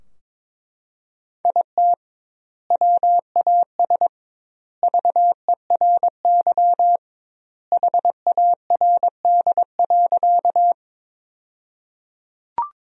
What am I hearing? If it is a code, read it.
Morse code, heard as a keyed tone, about 22 wpm, 700 Hz. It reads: IT WAS VERY HARD.
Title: 2x Word-Spacing - Sentences from Top 300 Words - 22wpm